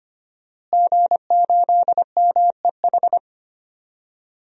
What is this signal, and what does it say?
Morse code, heard as a keyed tone, about 25 wpm, 700 Hz. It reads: G8MEH
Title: DXCC-100 Call Signs - 5 Characters - Repeated 3x - 25wpm